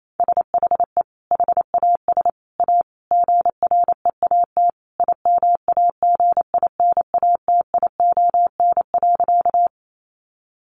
Morse code, heard as a keyed tone, about 28 wpm, 700 Hz. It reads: SHE HAS A GREAT IMAGINATION.
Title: Sentences from Top 1300 Words 28wpm